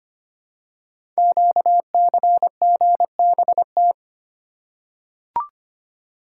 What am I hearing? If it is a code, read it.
Morse code, heard as a keyed tone, about 25 wpm, 700 Hz. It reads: QCGBT
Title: Code Groups - 5 Letter:Numbers - 25wpm